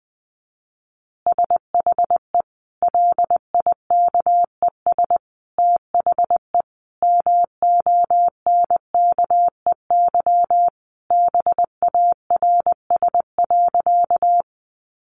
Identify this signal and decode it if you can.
Morse code, heard as a keyed tone, about 20 wpm, 700 Hz. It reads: SHE LIKES THE MONKEY BARS.
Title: Easy Phrases 20wpm